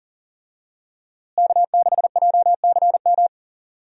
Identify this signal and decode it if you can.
Morse code, heard as a keyed tone, about 40 wpm, 700 Hz. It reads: K6JCM
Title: USA Call Signs - 5 Characters - Repeated 2x - 40wpm